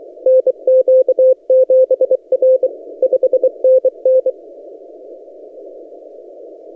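Morse code, heard as a keyed tone, about 24 wpm, 520 Hz. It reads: NQ7R 5NN
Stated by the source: XR0YG DX signal on 17 metres CW, posted by N6ML.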